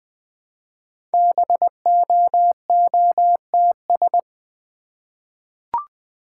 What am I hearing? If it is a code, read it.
Morse code, heard as a keyed tone, about 21 wpm, 690 Hz. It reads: BOOTS
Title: Easy Words with Hints - 20wpm